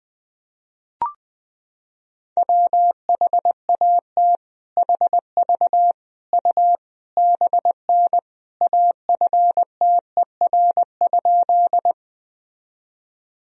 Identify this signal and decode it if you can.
Morse code, heard as a keyed tone, about 21 wpm, 700 Hz. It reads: WHAT HV U BN AFTER?
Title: Sentences from Top 200 Words and Abbreviations 20wpm